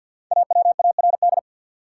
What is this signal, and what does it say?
Morse code, heard as a keyed tone, about 53 wpm, 700 Hz. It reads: AWARD
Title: Five Letter Words - 50wpm